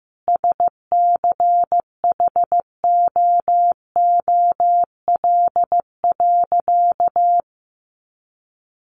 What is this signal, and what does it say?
Morse code, heard as a keyed tone, about 15 wpm, 700 Hz. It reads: SCHOOL.
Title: Additional Easy Phrases 15wpm